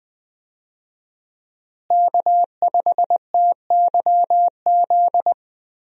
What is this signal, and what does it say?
Morse code, heard as a keyed tone, about 20 wpm, 700 Hz. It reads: K5TYZ
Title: USA Call Signs - 5 Characters - Repeated 3x - 20wpm